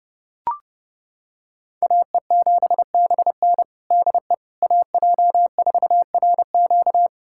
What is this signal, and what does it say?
Morse code, heard as a keyed tone, about 30 wpm, 700 Hz. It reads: AE7BN DE AJ4RQ
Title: USA Callsign DE Callsign - 5 Characters - 30wpm